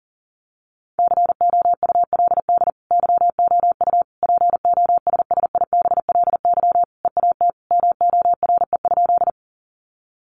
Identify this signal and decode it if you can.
Morse code, heard as a keyed tone, about 40 wpm, 700 Hz. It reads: COULD YOU POSSIBLY EAT MORE?
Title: Sentences from Top 1300 Words 40wpm